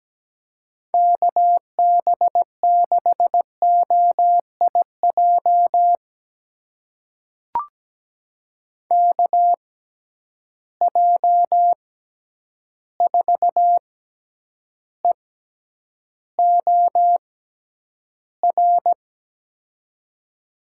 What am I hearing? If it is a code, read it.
Morse code, heard as a keyed tone, about 17 wpm, spaced as for 9 wpm, 700 Hz. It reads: KB6OIJ KJ4EOR
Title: USA Call Signs - 6 Characters - 6x Character Spacing - 17wpm